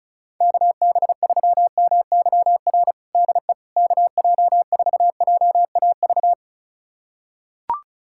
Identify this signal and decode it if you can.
Morse code, heard as a keyed tone, about 34 wpm, 700 Hz. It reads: KB3MYR DE KJ4JAV
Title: USA Callsign DE Callsign - 6 Characters - 35wpm